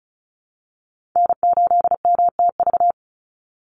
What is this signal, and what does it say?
Morse code, heard as a keyed tone, about 34 wpm, 700 Hz. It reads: N8MTV
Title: Call Signs Repeated 3x - 35wpm